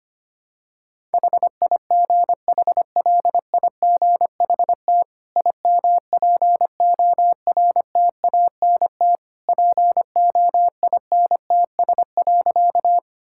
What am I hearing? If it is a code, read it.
Morse code, heard as a keyed tone, about 25 wpm, 700 Hz. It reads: HIGHLIGHT IMPORTANT POINTS.
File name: Easy Sentences with Hints - 25wpm